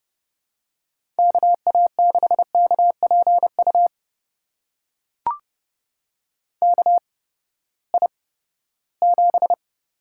Morse code, heard as a keyed tone, about 30 wpm, spaced as for 17 wpm, 700 Hz. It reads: KA6KPU KI7
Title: USA Call Signs - 6 Characters - 8x Character Spacing - 30wpm